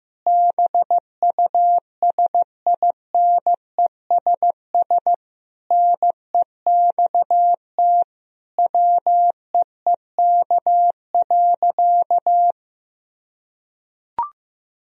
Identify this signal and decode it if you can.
Morse code, heard as a keyed tone, about 15 wpm, 700 Hz. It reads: BUSINESS NEXT WEEK.
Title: Sentences from Top 1000 Words - Review Part 4 - 15wpm